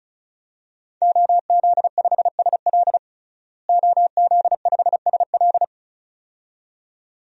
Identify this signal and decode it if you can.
Morse code, heard as a keyed tone, about 34 wpm, 700 Hz. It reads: OZ5SL OZ5SL
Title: DXCC-100 Call Signs - 5 Characters - Repeated 2x - 35wpm